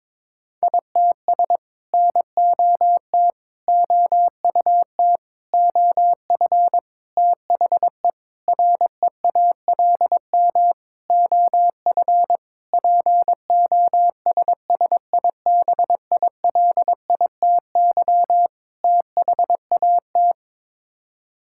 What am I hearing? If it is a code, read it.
Morse code, heard as a keyed tone, about 22 wpm, 700 Hz. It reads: ITS NOT OUT OF THE REALM OF POSSIBILITY THAT
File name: Sets of 9 Words - 22wpm